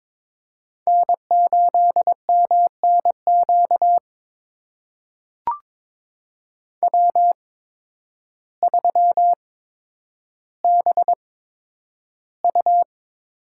Morse code, heard as a keyed tone, about 22 wpm, spaced as for 10 wpm, 700 Hz. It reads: N8MNQ W3BU
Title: USA Call Signs - 8x Character Spacing - 22wpm